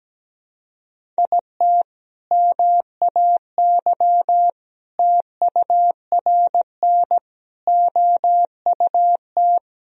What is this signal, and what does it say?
Morse code, heard as a keyed tone, about 17 wpm, 700 Hz. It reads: IT MAY TURN OUT
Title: Sentences from Top 1000 Words - Review Part 2 - 17wpm